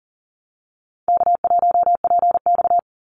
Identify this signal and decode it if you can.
Morse code, heard as a keyed tone, about 40 wpm, 700 Hz. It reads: K1PX
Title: Call Signs - 40wpm